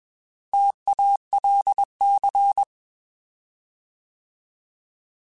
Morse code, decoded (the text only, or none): TALC